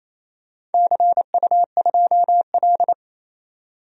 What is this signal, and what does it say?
Morse code, heard as a keyed tone, about 27 wpm, 700 Hz. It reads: CU2L
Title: DXCC-100 Call Signs - 4 Characters - 28wpm